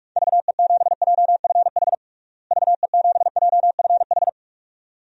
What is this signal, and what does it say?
Morse code, heard as a keyed tone, about 42 wpm, 700 Hz. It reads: VE7JFH VE7JFH
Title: DXCC-100 Call Signs - 6 Characters - Repeated 2x - 45wpm